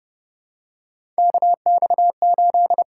Morse code, heard as a keyed tone, about 30 wpm, 700 Hz. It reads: KX8
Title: USA Call Signs - 6x Character Spacing - 30wpm